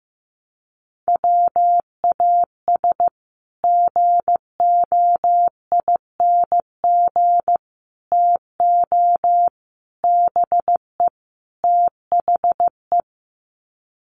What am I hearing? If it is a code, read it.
Morse code, heard as a keyed tone, about 15 wpm, 700 Hz. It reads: WAS GOING TO BE THE
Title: Sets of 5 Words Encore2 15wpm